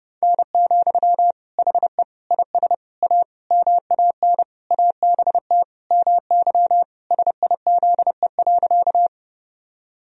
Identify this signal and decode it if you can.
Morse code, heard as a keyed tone, about 30 wpm, 700 Hz. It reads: N, HE IS A MAN ABT MY SIZE.
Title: Sentences from Top 400 Words and Abbreviations 30wpm